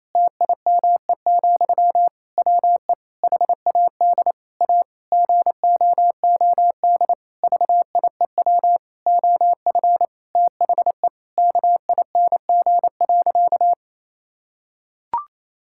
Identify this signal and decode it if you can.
Morse code, heard as a keyed tone, about 28 wpm, 700 Hz. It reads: TIME, WE HAD A GOOD VIEW OF THE KING.
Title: Sentences from Top 1000 Words - Review Part 6 - 28wpm